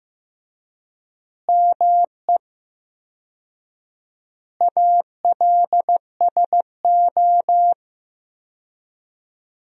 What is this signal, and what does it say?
Morse code, heard as a keyed tone, about 15 wpm, 700 Hz. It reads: ME ALSO
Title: 4x Word-Spacing - Sentences from Top 400 Words - 15wpm